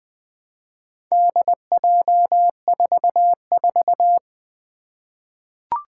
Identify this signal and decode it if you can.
Morse code, heard as a keyed tone, about 20 wpm, 700 Hz. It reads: DJ44
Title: Four Challenging Letter-Number Combinations 20wpm